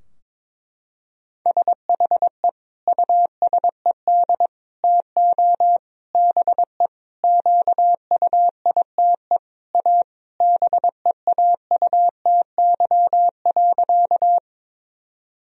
Morse code, heard as a keyed tone, about 22 wpm, 700 Hz. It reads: SHE USED TO BE QUITE A BEAUTY.
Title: Sentences from Top 1000 Words - Review Part 2 - 22wpm